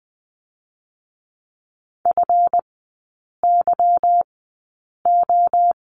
Morse code, heard as a keyed tone, about 20 wpm, spaced as for 8 wpm, 700 Hz. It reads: FYO
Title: 205 Lesson 33e - Letter j - Three Characters 20wpm